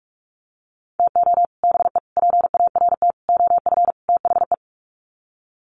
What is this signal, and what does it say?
Morse code, heard as a keyed tone, about 44 wpm, 700 Hz. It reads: TO BE PART OF THE
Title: Sets of 5 Words 45wpm